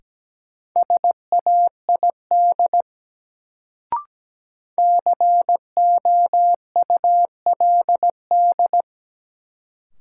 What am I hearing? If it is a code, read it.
Morse code, heard as a keyed tone, about 17 wpm, 700 Hz. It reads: SAID COULD
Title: Sentences from Top 100 Words - New Word Drill - 17wpm